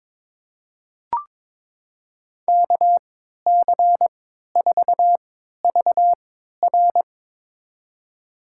Morse code, heard as a keyed tone, about 21 wpm, spaced as for 12 wpm, 700 Hz. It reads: KC4VR
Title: USA Call Signs - 5 Characters - 3x Character Spacing - 22wpm